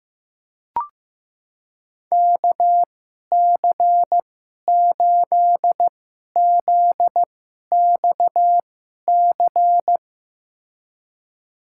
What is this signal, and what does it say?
Morse code, heard as a keyed tone, about 15 wpm, spaced as for 11 wpm, 700 Hz. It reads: KC8ZXC
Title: USA Call Signs - 6 Characters - 2x Character Spacing - 15wpm